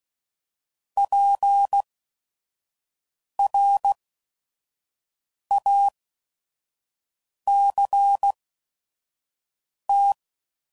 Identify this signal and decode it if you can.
Morse code, heard as a keyed tone, about 16 wpm, spaced as for 5 wpm, 800 Hz. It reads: PRACT
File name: CW Code Groups @5WPM / QSOs / QSO WD0FBT